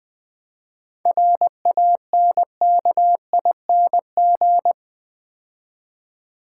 Morse code, heard as a keyed tone, about 20 wpm, 700 Hz. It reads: RANKING
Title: 7 Letter Words - First 3 Letters Given - 20wpm